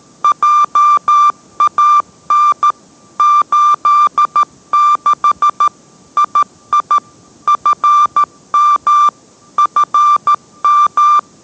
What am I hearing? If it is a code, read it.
Morse code, heard as a keyed tone, about 13 wpm, 1180 Hz. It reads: JAN 86 II FM FM